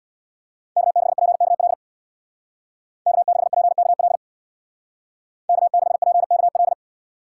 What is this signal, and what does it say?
Morse code, heard as a keyed tone, about 59 wpm, 700 Hz. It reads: K6PKL K6PKL K6PKL
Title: Speed Racing - USA 1x3 Call Signs 50wpm